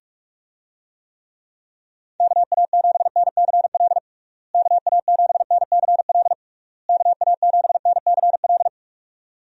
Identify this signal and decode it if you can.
Morse code, heard as a keyed tone, about 48 wpm, 700 Hz. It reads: KA7NCL KA7NCL KA7NCL
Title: USA Call Signs - 6 Characters - Repeated 3x - 45wpm